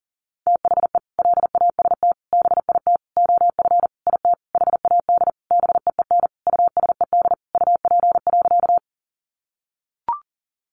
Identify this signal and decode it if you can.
Morse code, heard as a keyed tone, about 40 wpm, 700 Hz. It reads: THE LAST BIT OF IT HAD BEEN USED UP.
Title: Sentences from Top 900 Words 40wpm